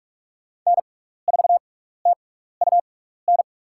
Morse code, heard as a keyed tone, about 46 wpm, spaced as for 16 wpm, 700 Hz. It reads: N4TUN
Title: USA Call Signs - 5 Characters - 6x Character Spacing - 45wpm